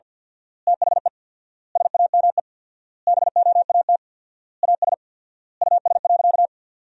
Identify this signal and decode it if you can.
Morse code, heard as a keyed tone, about 47 wpm, 700 Hz. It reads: THE SAME BOAT AS US.
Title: 4x Word-Spacing - Sentences from Top 500 Words - 50wpm